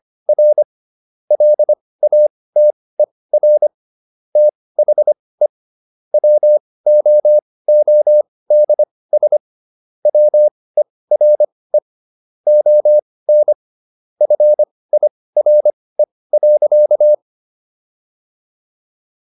Morse code, heard as a keyed tone, about 25 wpm, spaced as for 18 wpm, 600 Hz. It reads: R LATER THE WOODS WERE ON FIRE.